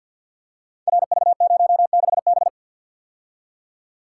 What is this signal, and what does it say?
Morse code, heard as a keyed tone, about 47 wpm, 700 Hz. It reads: RV06B
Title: Code Groups - 5 Letter:Numbers - 50wpm